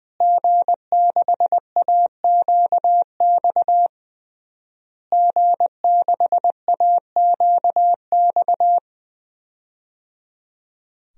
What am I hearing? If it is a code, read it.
Morse code, heard as a keyed tone, about 20 wpm, 700 Hz. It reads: G6AQX G6AQX
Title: DXCC-100 Call Signs - 5 Characters - Repeated 3x - 20wpm